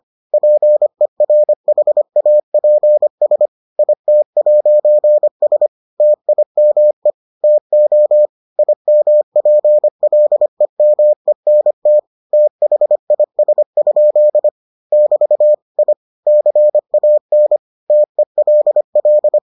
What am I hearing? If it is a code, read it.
Morse code, heard as a keyed tone, about 25 wpm, 600 Hz. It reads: PERHAPS IT'S TIME TO IMPLEMENT THIS? <BT> I CAN TELL